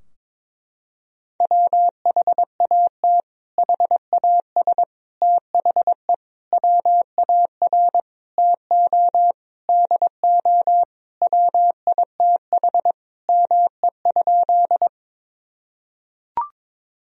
Morse code, heard as a keyed tone, about 22 wpm, 700 Hz. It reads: WHAT HAS THE WAR TO DO WITH ME?
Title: Sentences from Top 1000 Words - Review Part 2 - 22wpm